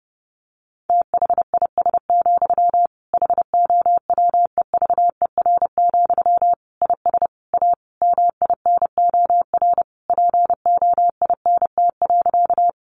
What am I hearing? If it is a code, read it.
Morse code, heard as a keyed tone, about 30 wpm, 700 Hz. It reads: THIS, HOWEVER, IS A MINOR POINT.
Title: Sentences from Top 1500 Words 30wpm